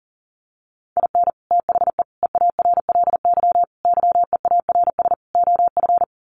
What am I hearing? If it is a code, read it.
Morse code, heard as a keyed tone, about 40 wpm, 700 Hz. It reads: IN THE EARLY YEARS OF
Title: Sets of 5 Words Encore 40wpm